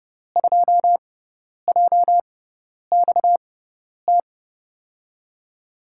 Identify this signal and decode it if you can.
Morse code, heard as a keyed tone, about 32 wpm, spaced as for 10 wpm, 700 Hz. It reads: 2JXT
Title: USA Call Signs - 6x Character Spacing - 30wpm